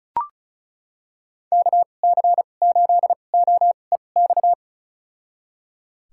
Morse code, heard as a keyed tone, about 36 wpm, spaced as for 25 wpm, 700 Hz. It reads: KC8OEX
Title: USA Call Signs - 2x Character Spacing - 35wpm